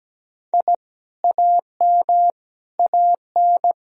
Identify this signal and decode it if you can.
Morse code, heard as a keyed tone, about 17 wpm, 700 Hz. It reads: I AM AN
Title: Sentences from Top 1000 Words - Review Part 2 - 17wpm